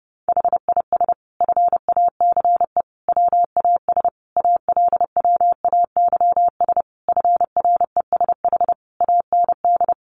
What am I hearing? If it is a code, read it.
Morse code, heard as a keyed tone, about 30 wpm, 700 Hz. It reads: HIS FACE WAS ALWAYS FRESH AND
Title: Sentences from Top 1200 Words 30wpm